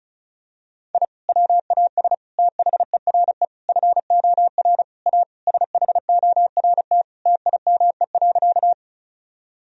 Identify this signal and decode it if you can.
Morse code, heard as a keyed tone, about 34 wpm, 700 Hz. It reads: I WAS THERE FOR A SHORT TIME.
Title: Sentences from Top 1000 Words - Review Part 2 - 35wpm